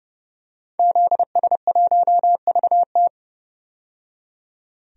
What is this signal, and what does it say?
Morse code, heard as a keyed tone, about 30 wpm, 700 Hz. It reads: ZS1VT
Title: DXCC-100 Call Signs - 5 Characters - 30wpm